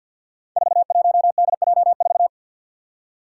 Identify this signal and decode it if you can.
Morse code, heard as a keyed tone, about 49 wpm, 700 Hz. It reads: 41DJ4
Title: Five Challenging Letter-Number Combinations 50wpm